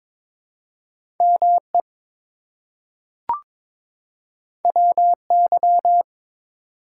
Morse code, heard as a keyed tone, about 21 wpm, 700 Hz. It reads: ME WY
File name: ARRL Sections v2 - 22wpm